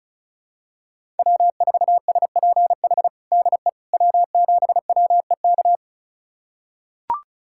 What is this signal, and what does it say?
Morse code, heard as a keyed tone, about 36 wpm, 700 Hz. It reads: W4SPH DE W7WEK
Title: USA Callsign DE Callsign - 5 Characters - 35wpm